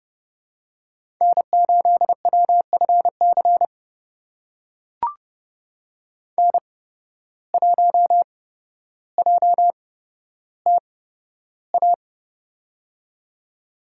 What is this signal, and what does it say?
Morse code, heard as a keyed tone, about 29 wpm, spaced as for 13 wpm, 700 Hz. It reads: N8WFC N1JTA
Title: USA Call Signs - 5 Characters - 8x Character Spacing - 30wpm